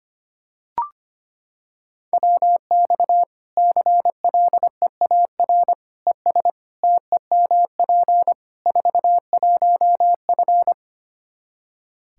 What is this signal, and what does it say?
Morse code, heard as a keyed tone, about 25 wpm, 700 Hz. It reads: WX CLEAR ES TEMP 41F
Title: K1USN SST QSO Practice 25wpm